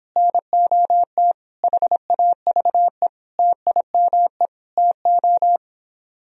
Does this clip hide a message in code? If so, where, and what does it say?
Yes: Morse code from 0.2 to 5.6 s, heard as a keyed tone, about 26 wpm, 700 Hz. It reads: NOT HAVE TIME TO